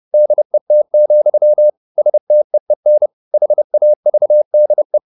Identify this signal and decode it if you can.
Morse code, heard as a keyed tone, about 30 wpm, 600 Hz. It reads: DET, STEEN HAVDE
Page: Ferie 30 wpm